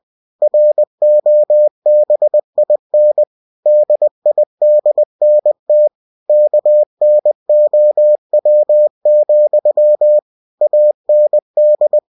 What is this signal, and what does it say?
Morse code, heard as a keyed tone, about 20 wpm, 600 Hz. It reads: ROBIN DIDNT KNOW, AND
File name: cw-books / winnie-the-pooh / winnie-20wpm-page-0106